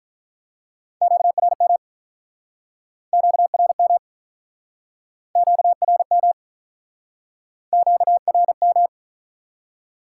Morse code, heard as a keyed tone, about 50 wpm, 700 Hz. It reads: QRM QRM QRM QRM